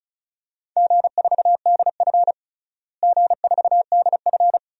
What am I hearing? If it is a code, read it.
Morse code, heard as a keyed tone, about 34 wpm, 700 Hz. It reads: G4DF G4DF
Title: DXCC-100 Call Signs - 4 Characters - Repeated 2x - 35wpm